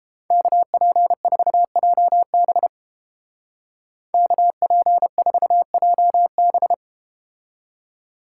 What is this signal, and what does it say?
Morse code, heard as a keyed tone, about 33 wpm, 700 Hz. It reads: KP4JB KP4JB